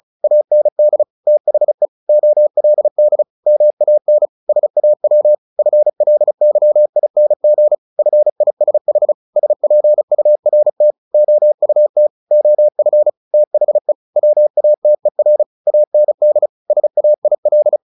Morse code, heard as a keyed tone, about 34 wpm, 600 Hz. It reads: AND THE OLD MAN SAW FLYING FISH SPURT OUT OF THE WATER AND SAIL
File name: the old man and the sea / the old man and the sea 1 35wpm